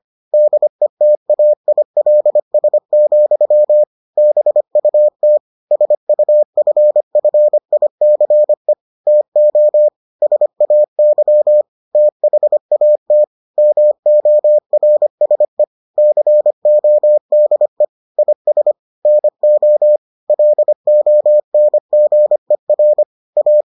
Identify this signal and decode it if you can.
Morse code, heard as a keyed tone, about 25 wpm, 600 Hz. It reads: DETAILS, BUT SUFFICE TO SAY THAT MORSE CODE IS NO LONGER A